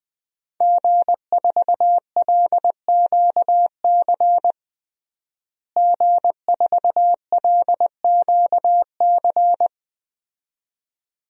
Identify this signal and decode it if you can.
Morse code, heard as a keyed tone, about 20 wpm, 700 Hz. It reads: G4LQC G4LQC
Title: DXCC-100 Call Signs - 5 Characters - Repeated 2x - 20wpm